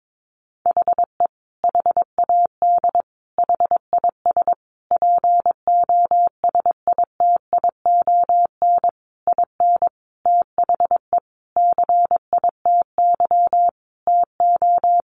Morse code, heard as a keyed tone, about 22 wpm, 700 Hz. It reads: HE HAD HIS POSITION IN THE CITY TO
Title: Sentences from Top 900 Words 22wpm